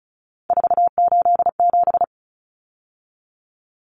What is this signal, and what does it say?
Morse code, heard as a keyed tone, about 34 wpm, 700 Hz. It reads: 487
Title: Numbers - 3-Digit 35wpm